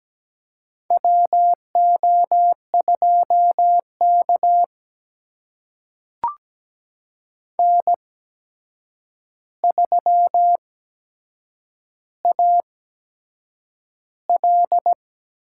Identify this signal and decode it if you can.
Morse code, heard as a keyed tone, about 17 wpm, spaced as for 7 wpm, 700 Hz. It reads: WO2K N3AL
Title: USA Call Signs - 4 Characters - 8x Character Spacing - 17wpm